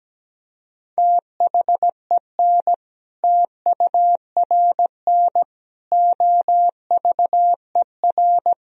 Morse code, heard as a keyed tone, about 17 wpm, 700 Hz. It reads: THEN TURN OVER
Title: Sentences from Top 1000 Words - Review Part 1 - 17wpm